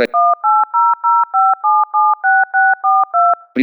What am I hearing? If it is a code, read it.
DTMF, heard as telephone touch tones, tones of about 370 ms, with speech. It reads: 18057642